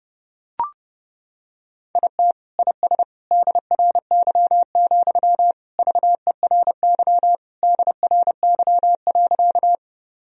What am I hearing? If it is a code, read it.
Morse code, heard as a keyed tone, about 32 wpm, 700 Hz. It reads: IT IS DRY, VERY DRY.
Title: Sentences from Top 1000 Words - Review Part 3 - 30wpm